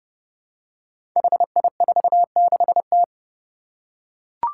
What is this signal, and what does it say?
Morse code, heard as a keyed tone, about 30 wpm, 700 Hz. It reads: HI46T
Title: Five Challenging Letter-Number Combinations 30wpm